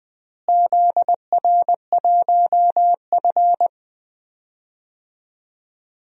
Morse code, heard as a keyed tone, about 20 wpm, 700 Hz. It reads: ZR1F
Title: DXCC-100 Call Signs - 4 Characters - Repeated 2x - 20wpm